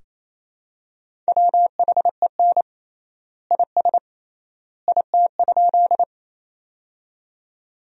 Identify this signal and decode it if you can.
Morse code, heard as a keyed tone, about 28 wpm, 700 Hz. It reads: WHEN IS IT?
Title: 3x Word-Spacing - Sentences from Top 100 Words - 28wpm